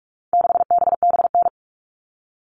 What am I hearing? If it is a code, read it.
Morse code, heard as a keyed tone, about 44 wpm, 700 Hz. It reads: 6BBN
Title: All Cut Numbers - 4-Digit 45wpm